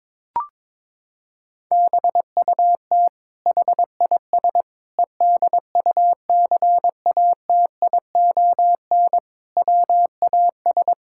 Morse code, heard as a keyed tone, about 22 wpm, 700 Hz. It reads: BUT HIS EDUCATION WAS
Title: Sentences from Top 1000 Words - Review Part 6 - 22wpm